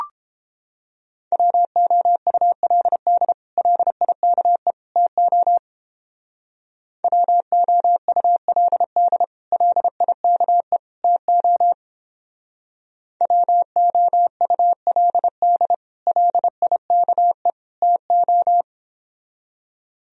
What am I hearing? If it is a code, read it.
Morse code, heard as a keyed tone, about 31 wpm, 700 Hz. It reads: WOULD LIKE TO WOULD LIKE TO WOULD LIKE TO